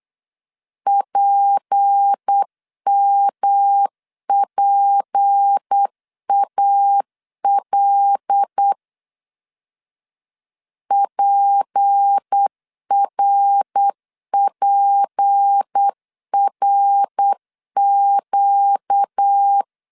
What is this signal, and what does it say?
Morse code, heard as a keyed tone, about 8 wpm, 800 Hz. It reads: PMPAL PRPRQ
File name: LEZIONI / 6 PKX0 / 6 P 8wpm 3025